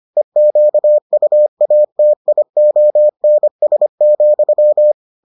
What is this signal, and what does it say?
Morse code, heard as a keyed tone, about 25 wpm, 600 Hz. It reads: EQUATIONS,